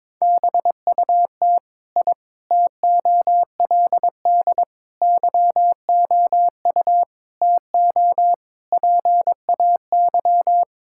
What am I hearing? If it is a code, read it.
Morse code, heard as a keyed tone, about 22 wpm, 700 Hz. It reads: BUT I TOLD YOU TO PAY
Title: Sentences from Top 1000 Words - Review Part 6 - 22wpm